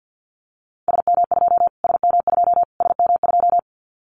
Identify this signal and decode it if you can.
Morse code, heard as a keyed tone, about 50 wpm, 700 Hz. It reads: SM2 SM2 SM2
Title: Callsign Prefixes for Top 25 DXCC - Repeated 3x - 50wpm